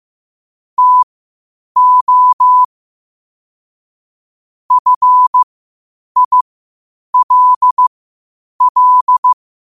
Morse code, heard as a keyed tone, about 15 wpm, spaced as for 8 wpm, 1000 Hz. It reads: TO FILL
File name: qotdfiles / 8wpm